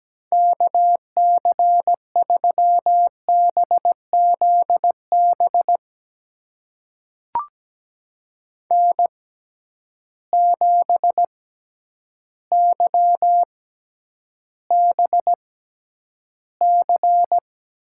Morse code, heard as a keyed tone, about 17 wpm, spaced as for 9 wpm, 700 Hz. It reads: KC3BZB N7YBC
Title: USA Call Signs - 6x Character Spacing - 17wpm